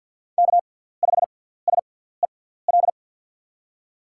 Morse code, heard as a keyed tone, about 46 wpm, spaced as for 17 wpm, 690 Hz. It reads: K5SEL
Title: USA Call Signs - 5 Characters - 6x Character Spacing - 50wpm